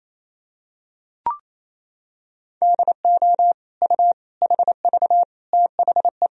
Morse code, heard as a keyed tone, about 28 wpm, 700 Hz. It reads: DO U HV THE
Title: Sentences from Top 300 Words and Abbreviations 28wpm